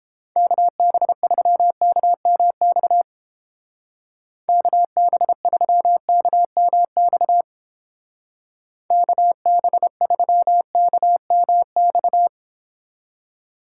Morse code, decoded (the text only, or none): KB3KMX KB3KMX KB3KMX